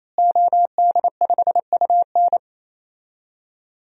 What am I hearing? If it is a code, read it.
Morse code, heard as a keyed tone, about 28 wpm, 700 Hz. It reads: OD5UN
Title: DXCC-100 Call Signs - 5 Characters - Repeated 3x - 28wpm